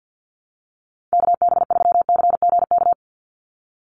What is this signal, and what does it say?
Morse code, heard as a keyed tone, about 51 wpm, 700 Hz. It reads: KB3CGK
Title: USA Call Signs 50wpm